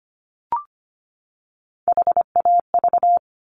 Morse code, heard as a keyed tone, about 25 wpm, 700 Hz. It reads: HAV
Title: Top Three Letter Combinations in Words 25wpm